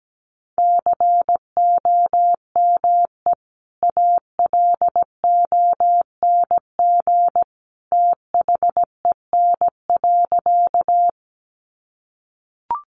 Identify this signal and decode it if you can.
Morse code, heard as a keyed tone, about 17 wpm, 700 Hz. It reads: COME ALONG THEN.
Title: Sentences from Top 700 Words 17wpm